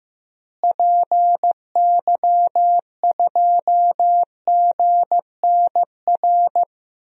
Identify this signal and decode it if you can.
Morse code, heard as a keyed tone, about 15 wpm, 700 Hz. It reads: PY2GNR
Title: DXCC-100 Call Signs - 6 Characters - Repeated 2x - 15wpm